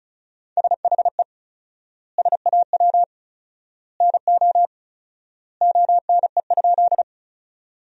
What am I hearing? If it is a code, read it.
Morse code, heard as a keyed tone, about 36 wpm, 700 Hz. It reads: SHE SAW NO ONE?
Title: 4x Word-Spacing - Sentences from Top 300 Words - 35wpm